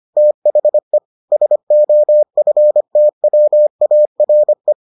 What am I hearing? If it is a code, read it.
Morse code, heard as a keyed tone, about 25 wpm, 600 Hz. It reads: THE SOFTWARE